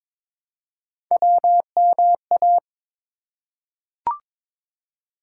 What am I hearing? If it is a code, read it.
Morse code, heard as a keyed tone, about 22 wpm, 700 Hz. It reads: WMA
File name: ARRL Sections v2 - 22wpm